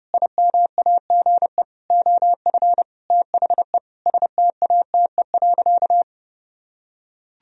Morse code, heard as a keyed tone, about 30 wpm, 700 Hz. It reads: IMAGE OF THE STATE.
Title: Sentences from Top 1000 Words - Review Part 4 - 30wpm